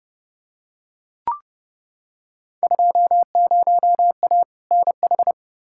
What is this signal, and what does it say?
Morse code, heard as a keyed tone, about 30 wpm, 700 Hz. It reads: 20A NH
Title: Field Day Exchanges Encore - 30wpm